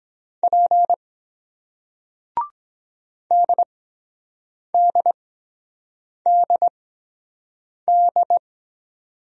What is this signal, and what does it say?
Morse code, heard as a keyed tone, about 22 wpm, spaced as for 7 wpm, 700 Hz. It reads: P DDDD